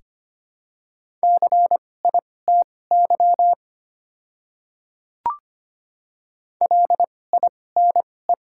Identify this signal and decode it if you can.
Morse code, heard as a keyed tone, about 25 wpm, spaced as for 18 wpm, 700 Hz. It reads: CITY LINE
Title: 2x Character Spacing - Top 300 Words - 25wpm